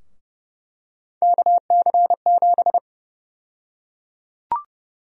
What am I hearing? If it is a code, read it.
Morse code, heard as a keyed tone, about 30 wpm, 700 Hz. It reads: KC7
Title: USA Call Sign Prefixes - 30wpm